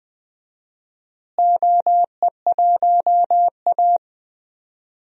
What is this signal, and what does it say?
Morse code, heard as a keyed tone, about 21 wpm, 700 Hz. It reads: OE1A
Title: DXCC-100 Call Signs - 4 Characters - Repeated 2x - 20wpm